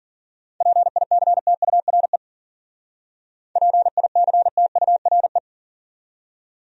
Morse code, heard as a keyed tone, about 52 wpm, 700 Hz. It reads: PICTURE PICTURE